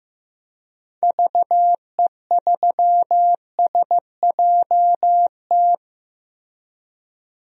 Morse code, heard as a keyed tone, about 16 wpm, 700 Hz. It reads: VE3SJT
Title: DXCC-100 Call Signs - 6 Characters - Repeated 3x - 15wpm